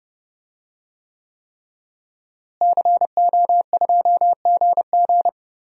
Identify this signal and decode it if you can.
Morse code, heard as a keyed tone, about 30 wpm, 700 Hz. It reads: CO2GG
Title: DXCC-100 Call Signs - 5 Characters - Repeated 3x - 30wpm